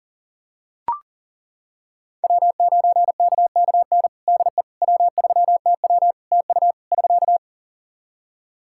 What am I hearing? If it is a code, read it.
Morse code, heard as a keyed tone, about 40 wpm, 700 Hz. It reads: W9KKN DE W3TW TU <SK>
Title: Practice QSO - Exchange 4 - Ending - 40wpm